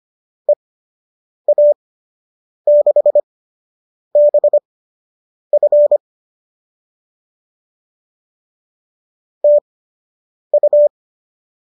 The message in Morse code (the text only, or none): EA6BF TU